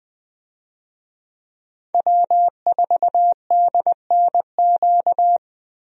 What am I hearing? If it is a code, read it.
Morse code, heard as a keyed tone, about 20 wpm, 700 Hz. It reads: W4DNQ
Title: USA Call Signs - 5 Characters - Repeated 3x - 20wpm